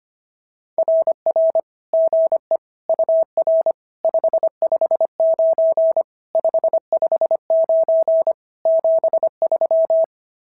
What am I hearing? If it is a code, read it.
Morse code, heard as a keyed tone, about 25 wpm, 650 Hz. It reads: RR GE UR 559 559 73
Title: SOTA QSO Practice 25wpm